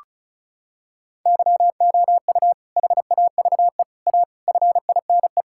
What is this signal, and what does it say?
Morse code, heard as a keyed tone, about 36 wpm, 700 Hz. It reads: YOU HAVE A FINE
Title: Sentences from Top 1000 Words - Review Part 6 - 35wpm